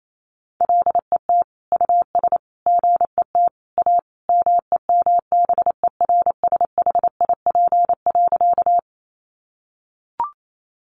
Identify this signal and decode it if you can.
Morse code, heard as a keyed tone, about 28 wpm, 700 Hz. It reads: LET US GET A MEMBERSHIP.
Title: Sentences from Top 1400 Words 28wpm